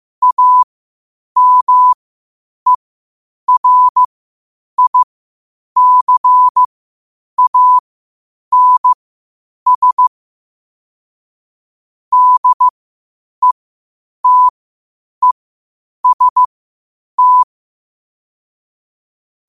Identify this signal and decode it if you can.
Morse code, heard as a keyed tone, about 15 wpm, spaced as for 8 wpm, 1000 Hz. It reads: AMERICANS DETEST